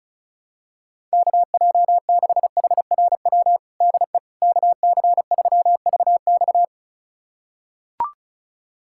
Morse code, heard as a keyed tone, about 36 wpm, 700 Hz. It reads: KJ6HRW DE KC3VX
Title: USA Callsign DE Callsign 35wpm